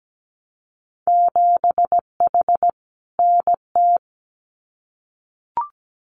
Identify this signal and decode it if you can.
Morse code, heard as a keyed tone, about 17 wpm, 700 Hz. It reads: 7H NT
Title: Winter Field Day Exchanges Encore 17wpm